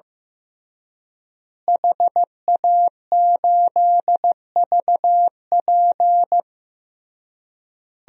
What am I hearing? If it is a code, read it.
Morse code, heard as a keyed tone, about 16 wpm, 700 Hz. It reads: HA8VP
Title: DXCC-100 Call Signs - 5 Characters - Repeated 3x - 15wpm